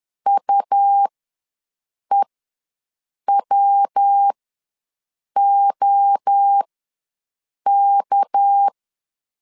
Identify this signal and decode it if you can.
Morse code, heard as a keyed tone, about 10 wpm, spaced as for 6 wpm, 800 Hz. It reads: UEWOK
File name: LEZIONI / 6 PKX0 / 6 PK ws 10wpm 3745